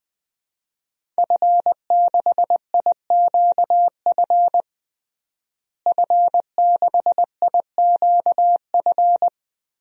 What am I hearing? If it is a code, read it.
Morse code, heard as a keyed tone, about 20 wpm, 700 Hz. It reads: F6IQF F6IQF
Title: DXCC-100 Call Signs - 5 Characters - Repeated 2x - 20wpm